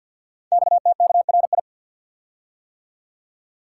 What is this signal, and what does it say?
Morse code, heard as a keyed tone, about 53 wpm, 700 Hz. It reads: XTKRI
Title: Code Groups - 5 Letters - 50wpm